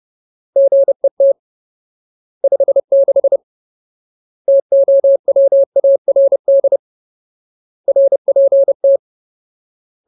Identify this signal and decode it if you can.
Morse code, heard as a keyed tone, about 30 wpm, 550 Hz. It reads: GET 56 TOWARD RPT
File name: Ultimate Word List / 30WPM Ultimate